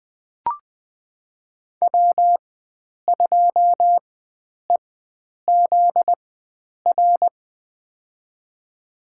Morse code, heard as a keyed tone, about 20 wpm, spaced as for 9 wpm, 700 Hz. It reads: W2EZR
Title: USA Call Signs - 5 Characters - 4x Character Spacing - 20wpm